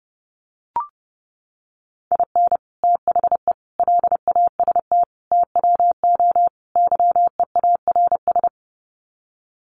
Morse code, heard as a keyed tone, about 30 wpm, 700 Hz. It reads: IN THE LAST TWO YEARS
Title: Sets of 5 Words 30wpm